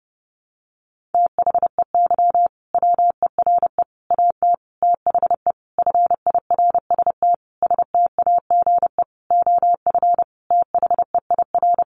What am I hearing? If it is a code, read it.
Morse code, heard as a keyed tone, about 30 wpm, 700 Hz. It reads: THEY WERE AT THE FIRST STAGE OF THEIR
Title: Sentences from Top 1200 Words 30wpm